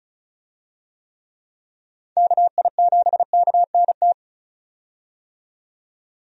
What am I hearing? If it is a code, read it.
Morse code, heard as a keyed tone, about 36 wpm, 700 Hz. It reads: KI7KNT
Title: USA Call Signs - 6 Characters - 35wpm